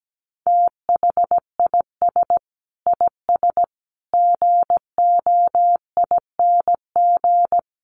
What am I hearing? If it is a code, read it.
Morse code, heard as a keyed tone, about 17 wpm, 700 Hz. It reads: THIS IS GOING
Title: Sets of 4 Words 17wpm